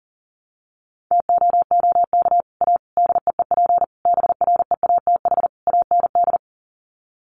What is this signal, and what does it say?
Morse code, heard as a keyed tone, about 40 wpm, 700 Hz. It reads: TOOK A DEEP BREATH AND
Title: Sets of 5 Words Encore 40wpm